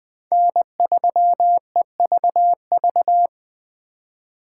Morse code, heard as a keyed tone, about 20 wpm, 700 Hz. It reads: N3EVV
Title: USA Call Signs - 5 Characters - 2x Character Spacing - 20wpm